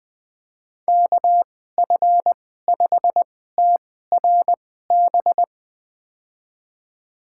Morse code, heard as a keyed tone, about 20 wpm, spaced as for 15 wpm, 700 Hz. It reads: KF5TRB
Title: USA Call Signs - 6 Characters - 2x Character Spacing - 20wpm